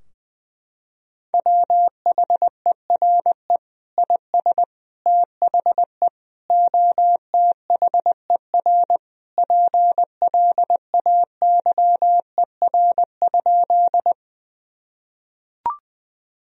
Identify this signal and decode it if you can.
Morse code, heard as a keyed tone, about 20 wpm, 700 Hz. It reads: WHERE IS THE OTHER PLAYER?
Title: Sentences from Top 1000 Words - Review Part 4 - 20wpm